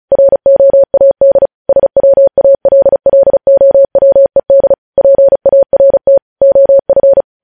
Morse code, heard as a keyed tone, about 36 wpm, 550 Hz. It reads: ROAD SWALLOWED PART OF
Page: news headlines in morse code at 35 wpm